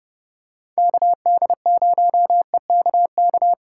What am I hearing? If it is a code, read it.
Morse code, heard as a keyed tone, about 30 wpm, 700 Hz. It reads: KD0EKK